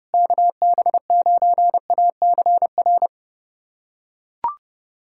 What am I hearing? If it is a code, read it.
Morse code, heard as a keyed tone, about 30 wpm, 700 Hz. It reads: KB9ACR
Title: USA Callsign DE Callsign - 6 Characters - 30wpm